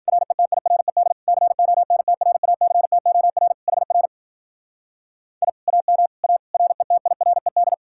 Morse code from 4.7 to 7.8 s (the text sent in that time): I AM A RETIRED